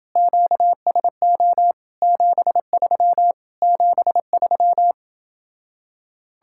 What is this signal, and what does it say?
Morse code, heard as a keyed tone, about 30 wpm, 700 Hz. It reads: QSO 73 73